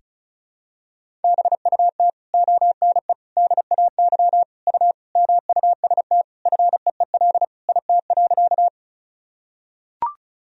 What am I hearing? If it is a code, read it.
Morse code, heard as a keyed tone, about 36 wpm, 700 Hz. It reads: BUT ONE DAY U MUST FEEL IT.
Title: Sentences from Top 400 Words and Abbreviations 35wpm